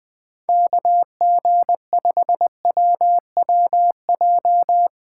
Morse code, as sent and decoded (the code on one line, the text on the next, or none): -.- --. ..... .-- .-- .---
KG5WWJ